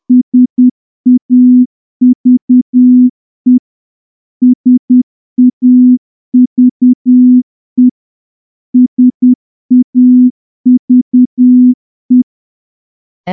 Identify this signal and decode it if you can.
Morse code, heard as a keyed tone, about 10 wpm, 250 Hz. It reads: SAVE SAVE SAVE
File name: morse-training-10WPM